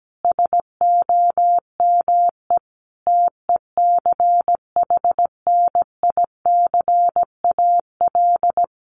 Morse code, heard as a keyed tone, about 17 wpm, 700 Hz. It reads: SOME TECHNICAL
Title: Sentences from Top 1100 Words 17wpm